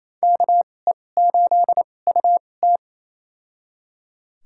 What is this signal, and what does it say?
Morse code, heard as a keyed tone, about 28 wpm, spaced as for 20 wpm, 700 Hz. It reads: KE8UT
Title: USA Call Signs - 2x Character Spacing - 28wpm